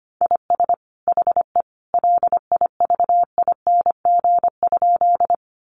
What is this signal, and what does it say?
Morse code, heard as a keyed tone, about 25 wpm, 700 Hz. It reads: IS HE LIVING?
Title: Sentences from Top 1100 Words 25wpm